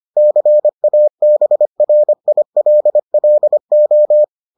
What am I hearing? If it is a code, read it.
Morse code, heard as a keyed tone, about 25 wpm, 600 Hz. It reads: CABRILLO